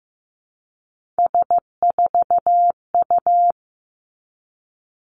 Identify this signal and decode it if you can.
Morse code, heard as a keyed tone, about 15 wpm, 700 Hz. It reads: S4U
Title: Three Challenging Letter-Number Combinations 15wpm